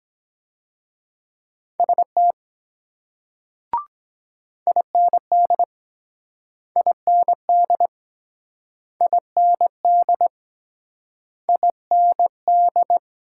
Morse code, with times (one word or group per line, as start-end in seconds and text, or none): 1.8-2.3 ST
4.7-5.6 IND
6.8-7.9 IND
9.0-10.3 IND
11.5-13.0 IND